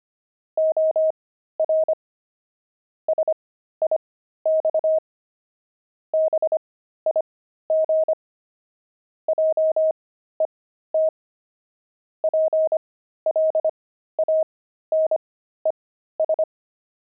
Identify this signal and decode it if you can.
Morse code, heard as a keyed tone, about 25 wpm, spaced as for 13 wpm, 630 Hz. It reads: OR SIX BIG JET PLANES
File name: pangram-fs-13-cs-25-jet